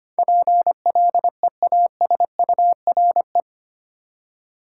Morse code, heard as a keyed tone, about 25 wpm, 700 Hz. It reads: PLEASURE
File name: Eight Letter Words - 25wpm